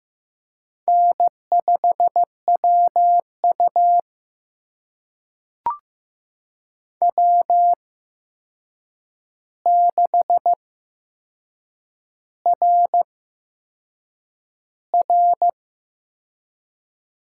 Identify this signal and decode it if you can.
Morse code, heard as a keyed tone, about 15 wpm, spaced as for 6 wpm, 700 Hz. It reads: N5WU W6RR
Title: USA Call Signs - 4 Characters - 8x Character Spacing - 15wpm